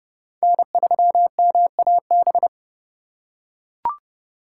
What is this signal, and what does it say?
Morse code, heard as a keyed tone, about 30 wpm, 700 Hz. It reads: N3MAB
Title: USA Call Signs - 4x Character Spacing - 30wpm